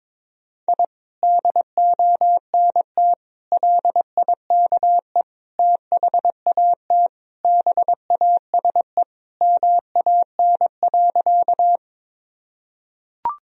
Morse code, heard as a keyed tone, about 22 wpm, 700 Hz. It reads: I DONT LIKE THAT BASE MAN.
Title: Sentences from Top 1000 Words - Review Part 1 - 22wpm